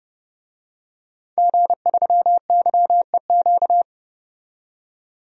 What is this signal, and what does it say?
Morse code, heard as a keyed tone, about 32 wpm, 700 Hz. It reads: G3YEQ
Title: DXCC-100 Call Signs - 5 Characters - 30wpm